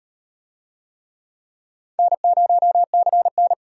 Morse code, heard as a keyed tone, about 36 wpm, 700 Hz. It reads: N0CN